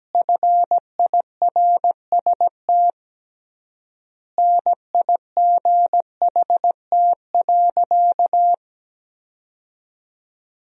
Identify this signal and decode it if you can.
Morse code, heard as a keyed tone, about 17 wpm, 700 Hz. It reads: FIRST NIGHT.
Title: 3x Word-Spacing - Sentences from Top 300 Words - 17wpm